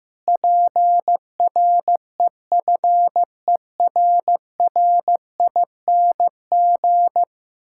Morse code, heard as a keyed tone, about 15 wpm, 700 Hz. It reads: PREFERRING
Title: Pangram Challenge v2 - 15wpm